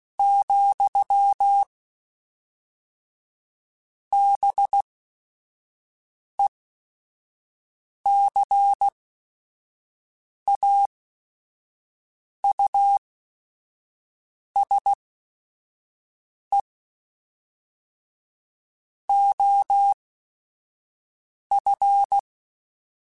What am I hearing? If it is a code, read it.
Morse code, heard as a keyed tone, about 16 wpm, spaced as for 5 wpm, 800 Hz. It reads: , BECAUSE OF
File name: Technical Subjects / Technical Subjects-mp3s / XC @5